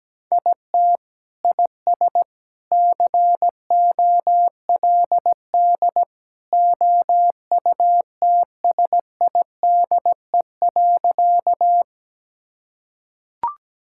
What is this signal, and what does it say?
Morse code, heard as a keyed tone, about 17 wpm, 700 Hz. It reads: IT IS COLD OUTSIDE.
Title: Sentences from Top 1000 Words - Review Part 6 - 17wpm